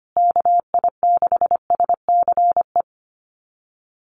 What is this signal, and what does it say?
Morse code, heard as a keyed tone, about 25 wpm, 700 Hz. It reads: KI6SCE
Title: USA Call Signs Repeated 2x 25wpm